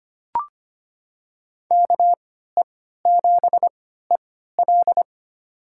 Morse code, heard as a keyed tone, about 25 wpm, spaced as for 14 wpm, 700 Hz. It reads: KE7EL
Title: USA Call Signs - 5 Characters - 3x Character Spacing - 25wpm